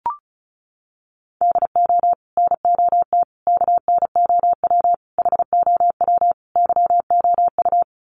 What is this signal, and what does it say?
Morse code, heard as a keyed tone, about 34 wpm, 700 Hz. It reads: DO NOT KNOW HOW YOU
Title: Sets of 5 Words Encore 35wpm